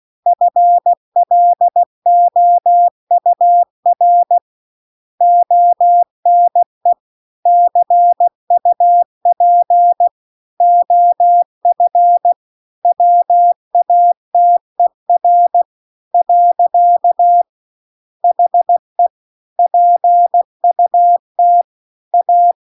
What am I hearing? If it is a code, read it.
Morse code, heard as a keyed tone, about 16 wpm, 700 Hz. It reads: FLOUR ONE CUP OF WATER . HE PUT A